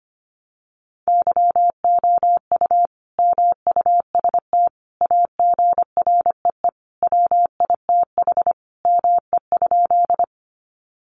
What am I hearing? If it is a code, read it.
Morse code, heard as a keyed tone, about 25 wpm, 700 Hz. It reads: YOU MUST AGREE WITH ME?
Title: Sentences from Top 1100 Words 25wpm